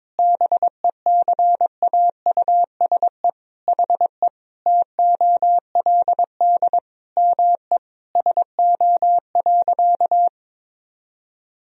Morse code, heard as a keyed tone, about 22 wpm, 700 Hz. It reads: BECAUSE HE TOLD ME SO.
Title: Sentences from Top 1000 Words - Review Part 3 - 22wpm